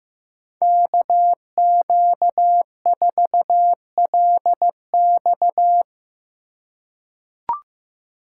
Morse code, heard as a keyed tone, about 16 wpm, 700 Hz. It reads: KQ4LX
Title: USA Call Signs - 5 Characters - 3x Character Spacing - 15wpm